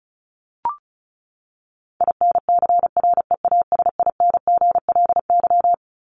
Morse code, heard as a keyed tone, about 34 wpm, 700 Hz. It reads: INCREASINGLY
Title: LY Suffix - Top 100 Words with Base Words 35wpm